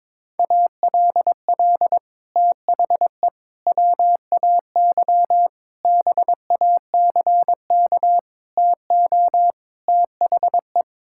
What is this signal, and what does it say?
Morse code, heard as a keyed tone, about 22 wpm, 700 Hz. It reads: ALL THE WAY BACK TO THE
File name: practice / 22wpm / Sets of 6 Words - 22wpm